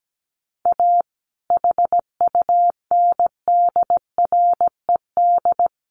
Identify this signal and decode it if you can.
Morse code, heard as a keyed tone, about 17 wpm, 700 Hz. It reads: A HUNDRED
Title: A Sound Of Thunder 17wpm